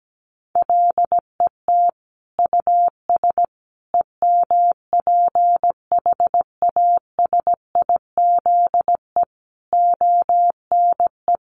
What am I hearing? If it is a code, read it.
Morse code, heard as a keyed tone, about 17 wpm, 700 Hz. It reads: LET US EMPHASIZE ONE
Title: Sentences from Top 1500 Words 17wpm